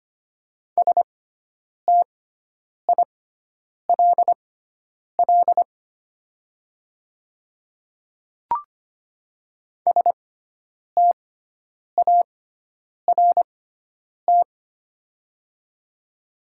Morse code, heard as a keyed tone, about 25 wpm, spaced as for 9 wpm, 700 Hz. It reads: STILL START
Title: 6x Character Spacing - Top 300 Words - 25wpm